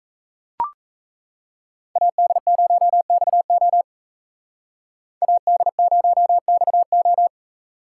Morse code, heard as a keyed tone, about 36 wpm, 700 Hz. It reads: AD0XO AD0XO